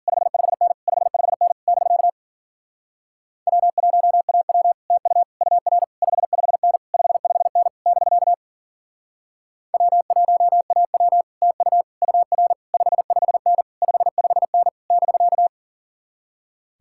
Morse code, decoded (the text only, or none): 55N 55N <BK> W1AW TU UR 55N 55N <BK> W1AW TU UR 55N 55N <BK>